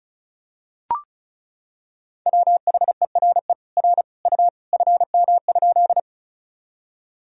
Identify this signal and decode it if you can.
Morse code, heard as a keyed tone, about 34 wpm, 700 Hz. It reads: WHERE R U FM?
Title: Sentences from Top 200 Words and Abbreviations 35wpm